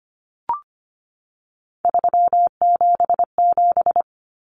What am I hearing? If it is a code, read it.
Morse code, heard as a keyed tone, about 25 wpm, 700 Hz. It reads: 377
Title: Numbers - 3-Digit 25wpm